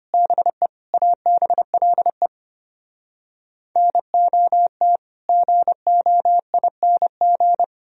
Morse code, heard as a keyed tone, about 33 wpm, 700 Hz. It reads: BE ABLE NOT GOING